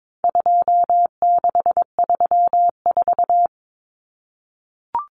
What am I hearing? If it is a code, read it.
Morse code, heard as a keyed tone, about 22 wpm, 700 Hz. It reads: 2634
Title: Numbers - 4-Digit 22wpm